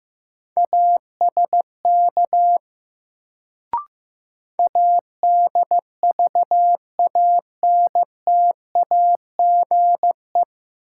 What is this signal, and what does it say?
Morse code, heard as a keyed tone, about 15 wpm, 700 Hz. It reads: ASK ADVANTAGE
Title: Sets of 4 Words - New Word Drill - 15wpm